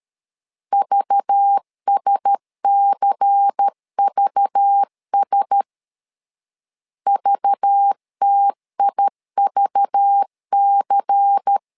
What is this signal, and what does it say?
Morse code, heard as a keyed tone, about 13 wpm, 800 Hz. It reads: VSCVS VTIVC
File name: LEZIONI / 5 LBVC / 5 LBVC 12wpm 1955